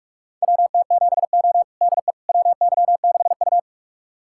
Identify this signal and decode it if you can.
Morse code, heard as a keyed tone, about 41 wpm, 700 Hz. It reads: WT7O DE WY6U
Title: USA Callsign DE Callsign - 4 Characters - 45wpm